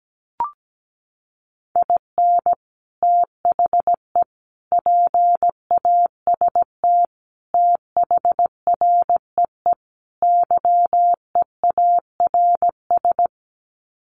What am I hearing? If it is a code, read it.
Morse code, heard as a keyed tone, about 17 wpm, 700 Hz. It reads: IN THE PAST THREE YEARS
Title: Sets of 5 Words Encore2 17wpm